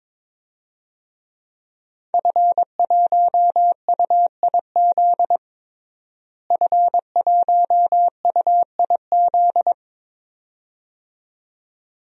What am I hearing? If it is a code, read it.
Morse code, heard as a keyed tone, about 21 wpm, 700 Hz. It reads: F1UIZ F1UIZ
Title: DXCC-100 Call Signs - 5 Characters - Repeated 2x - 22wpm